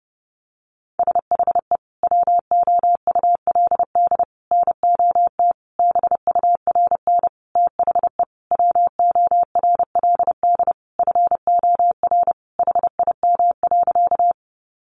Morse code, heard as a keyed tone, about 28 wpm, 700 Hz. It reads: SHE WOULD NOT BURN THE WORLD FOR HIM.
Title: Sentences from Top 1300 Words 30wpm